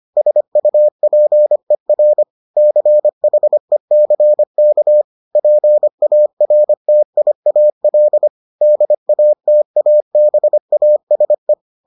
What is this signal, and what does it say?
Morse code, heard as a keyed tone, about 25 wpm, 600 Hz. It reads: SUPER CHECK PARTIAL DATABASE